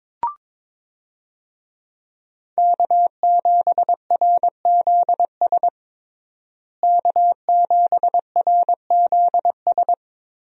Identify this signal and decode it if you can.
Morse code, heard as a keyed tone, about 22 wpm, 700 Hz. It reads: K7RZS K7RZS
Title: USA Call Signs - 5 Characters - Repeated 3x - 22wpm